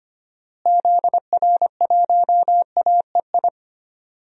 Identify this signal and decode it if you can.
Morse code, heard as a keyed tone, about 25 wpm, 700 Hz. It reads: ZR1AEI
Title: DXCC-100 Call Signs - 6 Characters - Repeated 3x - 25wpm